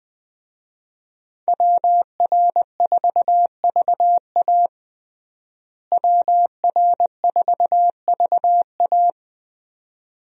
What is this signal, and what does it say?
Morse code, heard as a keyed tone, about 21 wpm, 700 Hz. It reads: WR4VA WR4VA
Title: USA Call Signs - 5 Characters - Repeated 2x - 20wpm